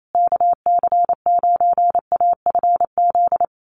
Morse code, heard as a keyed tone, about 28 wpm, 700 Hz. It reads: KC9AFZ
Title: USA Call Signs Repeated 3x 28wpm